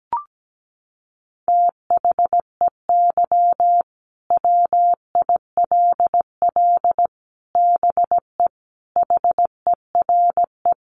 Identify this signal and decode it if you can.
Morse code, heard as a keyed tone, about 17 wpm, 700 Hz. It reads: THEY WILL BE HERE
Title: Sentences from Top 600 Words 17wpm